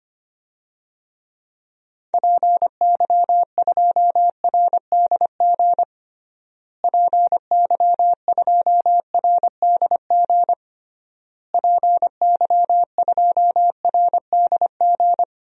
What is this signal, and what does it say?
Morse code, heard as a keyed tone, about 25 wpm, 700 Hz. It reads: PY2RDG PY2RDG PY2RDG
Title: DXCC-100 Call Signs - 6 Characters - Repeated 3x - 25wpm